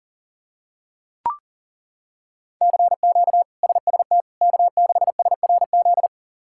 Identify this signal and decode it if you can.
Morse code, heard as a keyed tone, about 40 wpm, 700 Hz. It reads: CQ SST K6SRZ
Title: K1USN SST QSO Practice - 40wpm